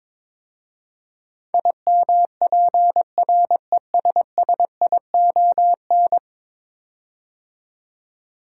Morse code, heard as a keyed tone, about 22 wpm, 700 Hz. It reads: IMPRESSION
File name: practice / 22wpm / 10 Letter Words - First 3 Letters Given - 22wpm